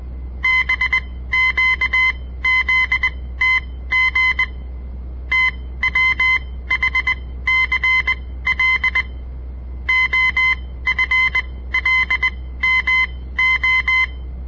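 Morse code, heard as a keyed tone, about 20 wpm, spaced as for 15 wpm, 2050 Hz. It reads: BQZTG TWHCL OFLMO
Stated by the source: On-air Morse code practice broadcast by a southern Montana amateur radio association; all code, no read back.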